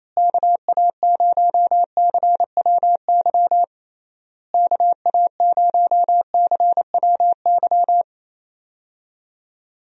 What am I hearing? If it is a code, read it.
Morse code, heard as a keyed tone, about 28 wpm, 700 Hz. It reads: KA0CWY KA0CWY
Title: USA Call Signs - 6 Characters - Repeated 3x - 28wpm